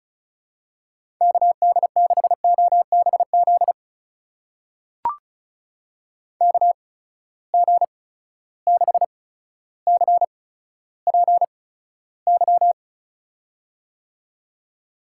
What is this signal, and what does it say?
Morse code, heard as a keyed tone, about 34 wpm, spaced as for 15 wpm, 700 Hz. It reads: KD6OBZ KG6CPY
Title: USA Call Signs - 6 Characters - 8x Character Spacing - 35wpm